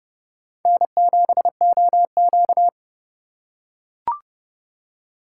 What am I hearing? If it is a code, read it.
Morse code, heard as a keyed tone, about 30 wpm, 700 Hz. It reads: N7OQ